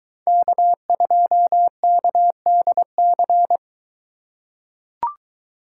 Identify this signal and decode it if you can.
Morse code, heard as a keyed tone, about 22 wpm, 700 Hz. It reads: K2KDC